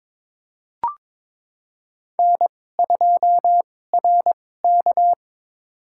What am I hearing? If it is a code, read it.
Morse code, heard as a keyed tone, about 22 wpm, spaced as for 16 wpm, 700 Hz. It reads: N2RK
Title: USA Call Signs - 4 Characters - 2x Character Spacing - 22wpm